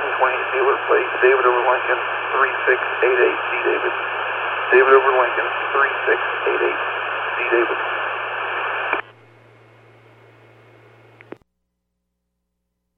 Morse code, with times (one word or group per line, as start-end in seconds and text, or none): none